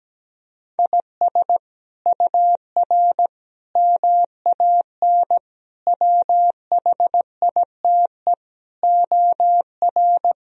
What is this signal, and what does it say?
Morse code, heard as a keyed tone, about 17 wpm, 700 Hz. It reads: IS UR MAN WHITE OR
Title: Sentences from Top 400 Words and Abbreviations 17wpm